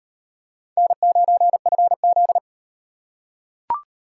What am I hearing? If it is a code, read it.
Morse code, heard as a keyed tone, about 38 wpm, 700 Hz. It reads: N9FZ